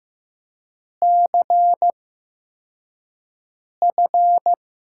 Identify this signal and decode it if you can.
Morse code, heard as a keyed tone, about 15 wpm, spaced as for 4 wpm, 700 Hz. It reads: CF
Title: USA Call Signs - 6 Characters - 8x Character Spacing - 15wpm